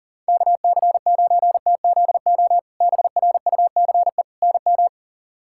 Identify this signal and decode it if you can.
Morse code, heard as a keyed tone, about 37 wpm, 700 Hz. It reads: KC9TZO BRUCE NM
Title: K1USN SST QSO Practice - 40wpm